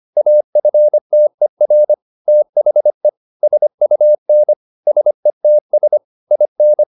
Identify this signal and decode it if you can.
Morse code, heard as a keyed tone, about 25 wpm, 600 Hz. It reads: AFTER THE SUN SETS IN